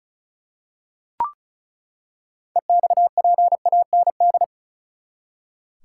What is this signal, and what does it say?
Morse code, heard as a keyed tone, about 36 wpm, 700 Hz. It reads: EXPAND
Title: Six Letter Words - 35wpm